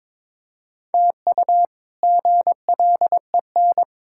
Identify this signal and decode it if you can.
Morse code, heard as a keyed tone, about 22 wpm, 700 Hz. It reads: TU GLEN
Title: K1USN SST QSO Practice - 22wpm